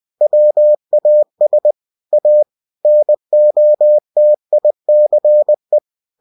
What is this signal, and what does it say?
Morse code, heard as a keyed tone, about 20 wpm, 600 Hz. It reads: WAS A NOTICE
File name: cw-books / winnie-the-pooh / winnie-20wpm-page-0029